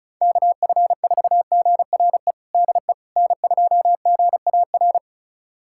Morse code, heard as a keyed tone, about 36 wpm, 700 Hz. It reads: KF4GRE DE N2GAR
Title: USA Callsign DE Callsign 35wpm